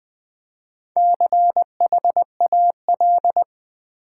Morse code, heard as a keyed tone, about 20 wpm, 700 Hz. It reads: CHAL
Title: Build A Word - 4 Letters At A Time - 11-Letter Words - 20wpm